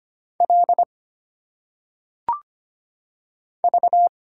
Morse code, heard as a keyed tone, about 25 wpm, 700 Hz. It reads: L V
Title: Single Letters - Phonetic Pronunciation 25wpm